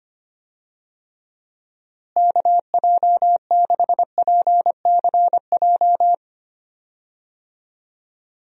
Morse code, heard as a keyed tone, about 25 wpm, 700 Hz. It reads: KJ6PCJ
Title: USA Call Signs - 6 Characters - Repeated 2x - 25wpm